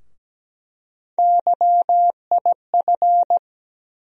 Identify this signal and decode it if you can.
Morse code, heard as a keyed tone, about 17 wpm, 700 Hz. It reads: YIF
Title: USA Call Sign Postfixes - 3 Letters 17wpm